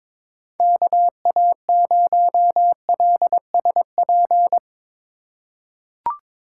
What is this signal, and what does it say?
Morse code, heard as a keyed tone, about 21 wpm, 700 Hz. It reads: KA0LSP
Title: USA Call Signs - 6 Characters - 3x Character Spacing - 22wpm